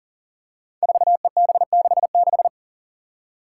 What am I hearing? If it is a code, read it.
Morse code, heard as a keyed tone, about 40 wpm, 690 Hz. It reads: 4EB66
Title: Five Challenging Letter-Number Combinations 40wpm